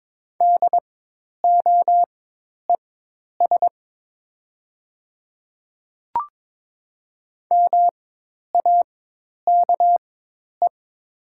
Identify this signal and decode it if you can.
Morse code, heard as a keyed tone, about 22 wpm, spaced as for 10 wpm, 700 Hz. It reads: DOES MAKE
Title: 4x Character Spacing - Top 200 Words - 22wpm